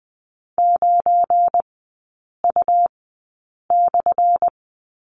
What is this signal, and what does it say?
Morse code, heard as a keyed tone, about 20 wpm, spaced as for 8 wpm, 700 Hz. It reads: 9U/
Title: 184 Lesson 30e - Letter b - Three Characters 20wpm